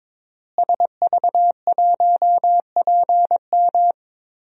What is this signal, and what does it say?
Morse code, heard as a keyed tone, about 22 wpm, 700 Hz. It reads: SV1PM
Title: DXCC-100 Call Signs - 5 Characters - Repeated 3x - 22wpm